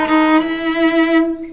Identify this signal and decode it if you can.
There is no code.